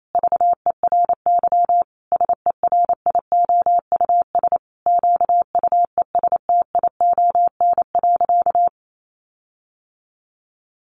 Morse code, heard as a keyed tone, about 28 wpm, 700 Hz. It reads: VERY SERIOUS QUESTION.
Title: Sentences from Top 1400 Words 28wpm